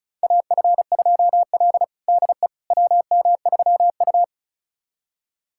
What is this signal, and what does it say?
Morse code, heard as a keyed tone, about 36 wpm, 700 Hz. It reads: AF2L DE WM3U
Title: USA Callsign DE Callsign - 4 Characters - 35wpm